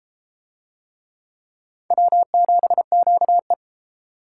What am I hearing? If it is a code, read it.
Morse code, heard as a keyed tone, about 31 wpm, 700 Hz. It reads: W7QE